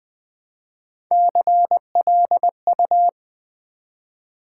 Morse code, heard as a keyed tone, about 20 wpm, 700 Hz. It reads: CLU
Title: Build A Word - 1 Letter At A Time - 4-Letter Words - 20wpm